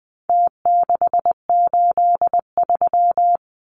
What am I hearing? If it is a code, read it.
Morse code, heard as a keyed tone, about 20 wpm, 700 Hz. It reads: T683
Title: Numbers - Contest Style - 4-Digit 20wpm